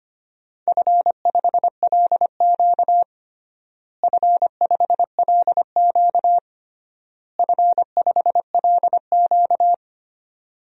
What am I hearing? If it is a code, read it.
Morse code, heard as a keyed tone, about 25 wpm, 700 Hz. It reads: F5LQ F5LQ F5LQ
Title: DXCC-100 Call Signs - 4 Characters - Repeated 3x - 25wpm